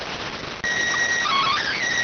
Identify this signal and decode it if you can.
SSTV (slow-scan television), PD 290.